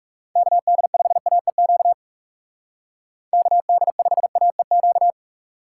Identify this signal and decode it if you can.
Morse code, heard as a keyed tone, about 38 wpm, 700 Hz. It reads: KD5AEQ KD5AEQ
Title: Speed Racing - USA 2x3 Call Signs 30wpm